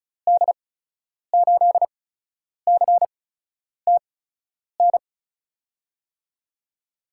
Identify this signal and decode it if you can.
Morse code, heard as a keyed tone, about 36 wpm, spaced as for 10 wpm, 700 Hz. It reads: D8CTN
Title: USA Call Signs - 6 Characters - 8x Character Spacing - 35wpm